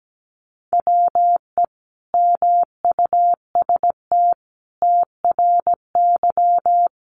Morse code, heard as a keyed tone, about 17 wpm, 700 Hz. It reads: WE MUST TRY
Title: Sentences from Top 1200 Words 17wpm